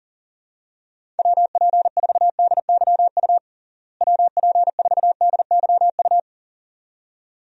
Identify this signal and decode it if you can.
Morse code, heard as a keyed tone, about 40 wpm, 700 Hz. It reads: WP4DYU WP4DYU
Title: USA Call Signs - 6 Characters - Repeated 2x - 40wpm